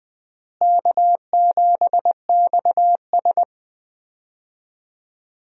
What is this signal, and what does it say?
Morse code, heard as a keyed tone, about 20 wpm, 700 Hz. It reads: K7XS
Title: USA Call Signs - 4 Characters - Repeated 2x - 20wpm